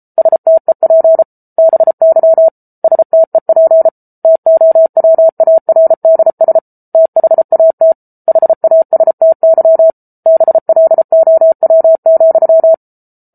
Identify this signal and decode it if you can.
Morse code, heard as a keyed tone, about 34 wpm, 650 Hz. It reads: STEP BY STEP TOWARDS THAT HASTY BLOW,